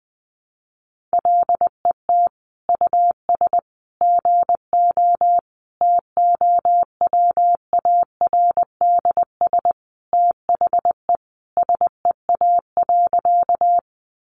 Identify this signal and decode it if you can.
Morse code, heard as a keyed tone, about 20 wpm, 700 Hz. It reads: LET US GO TOWARDS THE SEA.
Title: Sentences from Top 1100 Words 20wpm